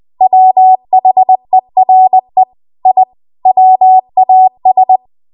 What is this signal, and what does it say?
Morse code, heard as a keyed tone, about 20 wpm, 750 Hz. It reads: WHERE I WAS
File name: Archive / 20 WPM / 20WPM